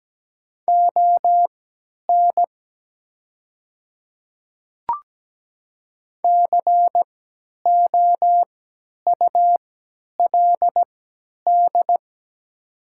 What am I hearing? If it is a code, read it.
Morse code, heard as a keyed tone, about 17 wpm, spaced as for 10 wpm, 700 Hz. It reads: ON COULD
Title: 3x Character Spacing - Top 100 Words - 17wpm